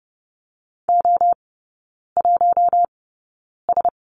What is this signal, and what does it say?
Morse code, heard as a keyed tone, about 30 wpm, spaced as for 9 wpm, 700 Hz. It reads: O1S
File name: 081 Lesson 14e - Number 2 - Three Characters 30wpm